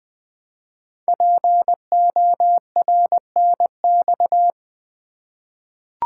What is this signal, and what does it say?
Morse code, heard as a keyed tone, about 20 wpm, 700 Hz. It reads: PORNX
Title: Code Groups - 5 Letters - 20wpm